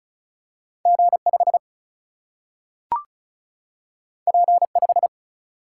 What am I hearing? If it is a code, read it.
Morse code, heard as a keyed tone, about 36 wpm, 700 Hz. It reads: G 5 P 5
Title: Single-Letter Single-Number 35wpm